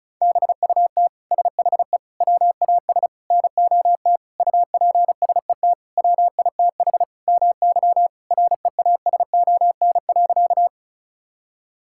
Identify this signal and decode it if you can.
Morse code, heard as a keyed tone, about 36 wpm, 700 Hz. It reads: BUT SHE WAS NOT UPSET WITH MY REASON.
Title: Sentences from Top 1000 Words - Review Part 5 - 35wpm